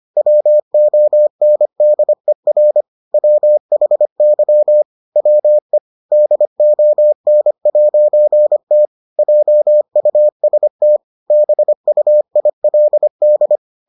Morse code, heard as a keyed tone, about 25 wpm, 600 Hz. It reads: WONDER WHY WE DON'T JUST BUILD